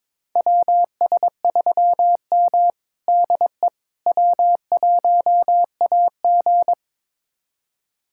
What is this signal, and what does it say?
Morse code, heard as a keyed tone, about 22 wpm, 700 Hz. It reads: WS3M DE W1AG
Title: USA Callsign DE Callsign - 4 Characters - 22wpm